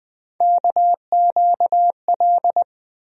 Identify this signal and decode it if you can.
Morse code, heard as a keyed tone, about 20 wpm, 700 Hz. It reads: KQL
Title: USA Call Sign Postfixes - 3 Letters 20wpm